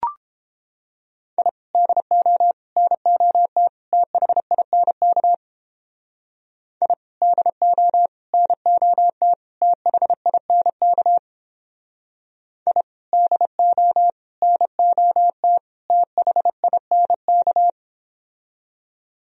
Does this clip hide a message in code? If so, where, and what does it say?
Yes: Morse code from 1.4 to 17.7 s, heard as a keyed tone, about 31 wpm, 700 Hz. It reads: I DO NOT THINK I DO NOT THINK I DO NOT THINK